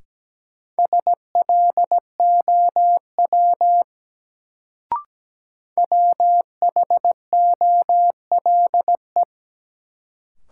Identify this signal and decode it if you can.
Morse code, heard as a keyed tone, about 17 wpm, 700 Hz. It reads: SLOW WHOLE
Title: Sentences from Top 400 Words - New Word Drill - 17wpm